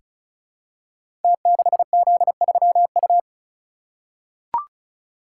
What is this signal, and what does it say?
Morse code, heard as a keyed tone, about 36 wpm, 700 Hz. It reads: T6Z3U
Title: Code Groups - 5 Letter:Numbers - 35wpm